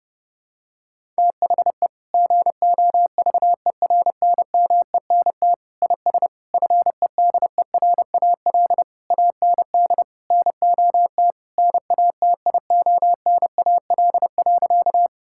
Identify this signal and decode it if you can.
Morse code, heard as a keyed tone, about 30 wpm, 700 Hz. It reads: THE GOVERNMENT IS FEDERAL AND NOT NATIONAL.
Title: Sentences from Top 1000 Words - Review Part 5 - 30wpm